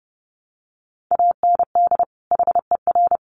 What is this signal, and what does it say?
Morse code, heard as a keyed tone, about 30 wpm, 700 Hz. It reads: AND HER
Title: Sets of 2 Words 30wpm